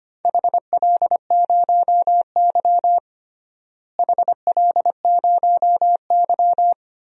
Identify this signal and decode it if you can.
Morse code, heard as a keyed tone, about 25 wpm, 700 Hz. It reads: HL0Y HL0Y
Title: DXCC-100 Call Signs - 4 Characters - Repeated 2x - 25wpm